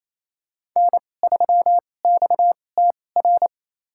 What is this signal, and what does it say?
Morse code, heard as a keyed tone, about 28 wpm, spaced as for 20 wpm, 700 Hz. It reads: N3XTR
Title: USA Call Signs - 5 Characters - 2x Character Spacing - 28wpm